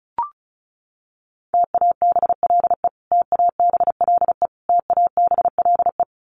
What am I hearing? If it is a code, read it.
Morse code, heard as a keyed tone, about 34 wpm, 700 Hz. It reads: TABLE TABLE TABLE
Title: Top 500 Words Repeated 35wpm